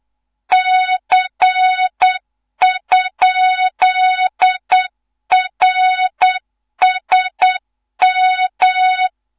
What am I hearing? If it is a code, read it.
Morse code, heard as a keyed tone, about 8 wpm, 3700 Hz, with noise. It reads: C?RSM
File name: Morze-HA2MN / hang / 15 GYAKORLAT